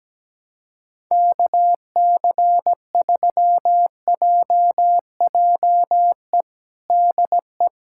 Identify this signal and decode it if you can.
Morse code, heard as a keyed tone, about 17 wpm, 700 Hz. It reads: KC3JJE DE
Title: USA Callsign DE Callsign 17wpm